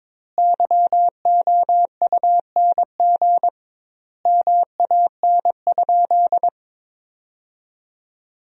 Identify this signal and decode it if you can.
Morse code, heard as a keyed tone, about 22 wpm, 700 Hz. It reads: YOUNG MAN?
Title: 2x Word-Spacing - Sentences from Top 400 Words - 22wpm